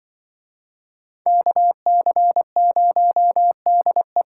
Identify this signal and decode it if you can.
Morse code, heard as a keyed tone, about 25 wpm, 700 Hz. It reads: KC0DE
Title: Speed Racing - USA 2x2 Call Signs 20wpm